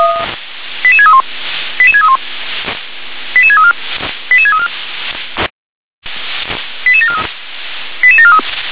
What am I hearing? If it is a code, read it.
Five-tone selective call (ZVEI), heard as quick runs of five tones, tones of about 70 ms. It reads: 88631 88631 88634 88634 88743 88743
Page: SDR/ZVEI